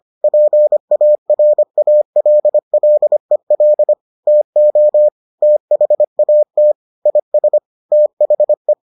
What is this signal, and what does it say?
Morse code, heard as a keyed tone, about 25 wpm, 600 Hz. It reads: PARALLEL TO THAT IS THE